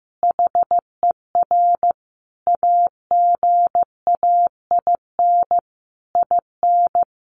Morse code, heard as a keyed tone, about 15 wpm, 700 Hz. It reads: HER AGAIN IN
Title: Sentences from Top 200 Words 15wpm